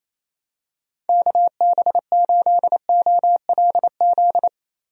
Morse code, heard as a keyed tone, about 28 wpm, 700 Hz. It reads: KB8OLZ
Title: USA Call Signs - 6 Characters - Repeated 3x - 28wpm